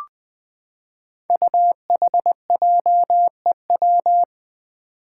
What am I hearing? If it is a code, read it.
Morse code, heard as a keyed tone, about 20 wpm, 700 Hz. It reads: UHJEW
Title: Five Challenging Letter-Number Combinations 20wpm